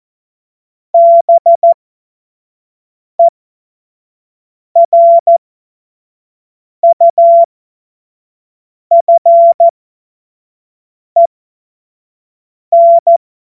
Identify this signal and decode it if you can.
Morse code, heard as a keyed tone, about 14 wpm, spaced as for 5 wpm, 680 Hz. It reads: BERUFEN
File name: Morsekurs / Witze / Witz 21. Juni 2022 25 F70